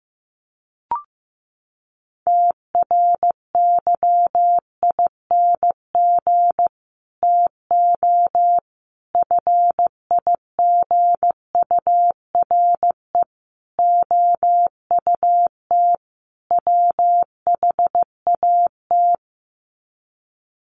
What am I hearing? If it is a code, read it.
Morse code, heard as a keyed tone, about 15 wpm, 700 Hz. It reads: TRYING TO FIGURE OUT WHAT
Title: Sets of 5 Words 15wpm